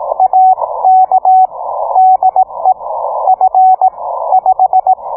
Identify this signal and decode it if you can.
Morse code, heard as a keyed tone, about 18 wpm, spaced as for 14 wpm, 750 Hz. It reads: AK DE F5